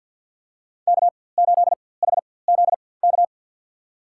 Morse code, heard as a keyed tone, about 53 wpm, spaced as for 24 wpm, 700 Hz. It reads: K8HZK
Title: USA Call Signs - 5 Characters - 4x Character Spacing - 50wpm